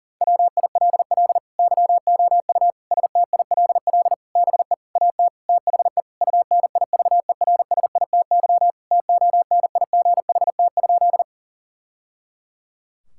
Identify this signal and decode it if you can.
Morse code, heard as a keyed tone, about 40 wpm, 700 Hz. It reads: WILL YOU STILL BE AT THE UNIVERSITY TONIGHT?
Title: Sentences from Top 1000 Words - Review Part 5 - 40wpm